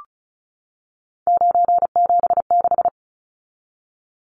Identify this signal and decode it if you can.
Morse code, heard as a keyed tone, about 34 wpm, 700 Hz. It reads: 976
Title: Numbers - 3-Digit 35wpm